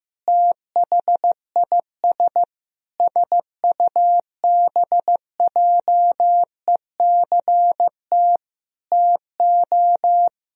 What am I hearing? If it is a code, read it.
Morse code, heard as a keyed tone, about 15 wpm, 700 Hz. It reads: THIS SUBJECT TO
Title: Sentences from Top 1000 Words - Review Part 1 - 15wpm